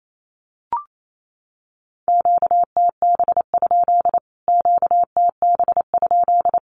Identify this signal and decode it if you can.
Morse code, heard as a keyed tone, about 28 wpm, 700 Hz. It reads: QTB? QTB?
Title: Q-Codes Repeated 3x 28wpm